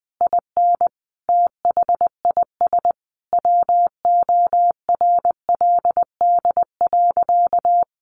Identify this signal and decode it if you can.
Morse code, heard as a keyed tone, about 20 wpm, 700 Hz. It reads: IN THIS WORLD.
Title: Sentences from Top 600 Words 20wpm